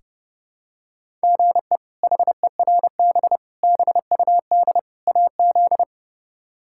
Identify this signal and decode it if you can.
Morse code, heard as a keyed tone, about 32 wpm, 700 Hz. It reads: GE HERB BUD AZ
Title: K1USN SST QSO Practice - 30wpm